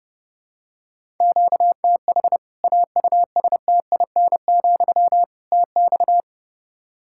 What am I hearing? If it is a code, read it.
Morse code, heard as a keyed tone, about 30 wpm, 700 Hz. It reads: QTH AUSTIN, TX
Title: Practice QSO - Exchange 1 - Beginning - 30wpm